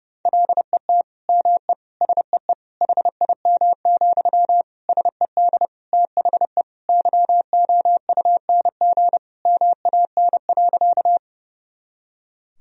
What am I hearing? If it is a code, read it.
Morse code, heard as a keyed tone, about 30 wpm, 700 Hz. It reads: LET ME SEE HIM, SED THE YOUNG MAN.
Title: Sentences from Top 400 Words and Abbreviations 30wpm